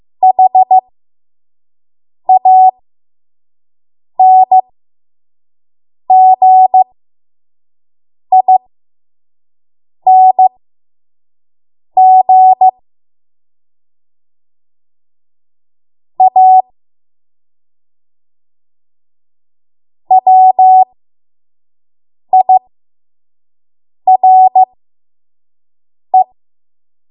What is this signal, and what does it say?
Morse code, heard as a keyed tone, about 15 wpm, spaced as for 5 wpm, 750 Hz. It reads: HANGING A WIRE
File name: Morse / 05WPM